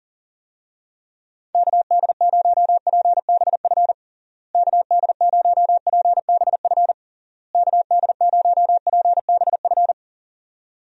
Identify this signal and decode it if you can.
Morse code, heard as a keyed tone, about 40 wpm, 700 Hz. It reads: KD0PBF KD0PBF KD0PBF
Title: USA Call Signs - 6 Characters - Repeated 3x - 40wpm